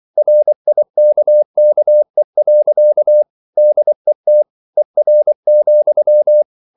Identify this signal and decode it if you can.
Morse code, heard as a keyed tone, about 24 wpm, 600 Hz. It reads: RIKKE. DET ER,
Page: Ferie 24 wpm